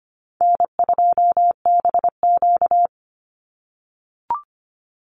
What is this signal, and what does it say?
Morse code, heard as a keyed tone, about 25 wpm, 700 Hz. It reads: N2BQ
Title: USA Call Signs 25wpm